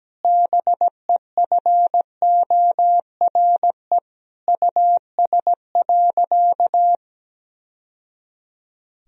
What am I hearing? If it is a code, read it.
Morse code, heard as a keyed tone, about 17 wpm, 700 Hz. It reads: BEFORE US.
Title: Sentences from Top 1000 Words - Review Part 1 - 17wpm